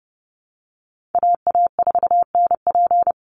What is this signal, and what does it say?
Morse code, heard as a keyed tone, about 30 wpm, 700 Hz. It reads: AA4NP
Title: Call Signs Repeated 2x - 30wpm